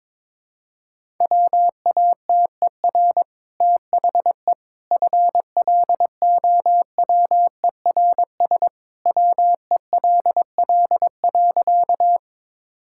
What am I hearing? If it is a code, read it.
Morse code, heard as a keyed tone, about 23 wpm, 700 Hz. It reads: WATER THE FLOWERS WELL.
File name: practice / 22wpm / Easy Sentences with Hints - 22wpm